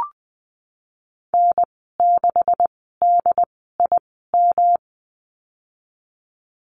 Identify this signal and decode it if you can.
Morse code, heard as a keyed tone, about 20 wpm, spaced as for 15 wpm, 700 Hz. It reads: N6DIM
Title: 167 Lesson 27g - Number 7 - Call Signs 20wpm